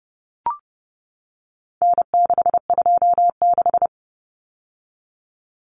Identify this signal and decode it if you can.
Morse code, heard as a keyed tone, about 30 wpm, 700 Hz. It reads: N626
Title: Common Cut Numbers - 4-Digit 30wpm